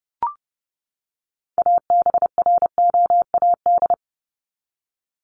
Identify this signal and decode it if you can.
Morse code, heard as a keyed tone, about 30 wpm, 700 Hz. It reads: ABROAD
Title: Top 1751-2000 Words 30wpm